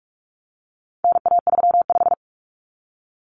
Morse code, heard as a keyed tone, about 44 wpm, 700 Hz. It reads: NA35
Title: Common Cut Numbers - 4-Digit 45wpm